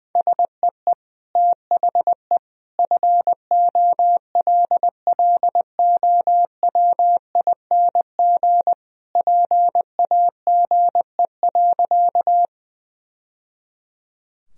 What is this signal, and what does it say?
Morse code, heard as a keyed tone, about 20 wpm, 700 Hz. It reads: SEE THE FOLLOWING PAGE.
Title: Sentences from Top 1000 Words - Review Part 1 - 20wpm